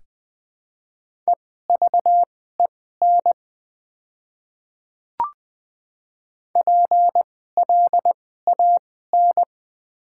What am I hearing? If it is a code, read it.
Morse code, heard as a keyed tone, about 20 wpm, spaced as for 15 wpm, 700 Hz. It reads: EVEN PLAN
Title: 2x Character Spacing - Top 500 Words - 20wpm